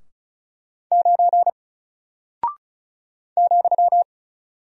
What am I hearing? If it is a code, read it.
Morse code, heard as a keyed tone, about 36 wpm, 700 Hz. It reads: 9 ,
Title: Single Character (Letter:Number:Punctuation:Prosign) 35wpm